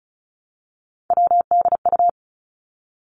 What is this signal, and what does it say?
Morse code, heard as a keyed tone, about 36 wpm, 700 Hz. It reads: WDU
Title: Three Challenging Letter-Number Combinations 35wpm